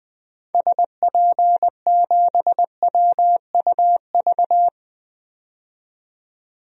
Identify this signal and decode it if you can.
Morse code, heard as a keyed tone, about 20 wpm, 700 Hz. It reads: SP7WUV
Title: DXCC-100 Call Signs - 6 Characters - 20wpm